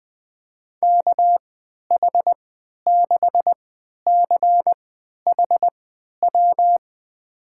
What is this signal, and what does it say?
Morse code, heard as a keyed tone, about 20 wpm, spaced as for 11 wpm, 700 Hz. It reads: KH6CHW
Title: USA Call Signs - 6 Characters - 3x Character Spacing - 20wpm